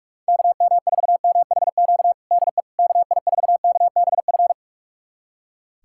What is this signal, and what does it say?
Morse code, heard as a keyed tone, about 44 wpm, 700 Hz. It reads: KM4MHQ DE KI4KBF
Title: USA Callsign DE Callsign - 6 Characters - 45wpm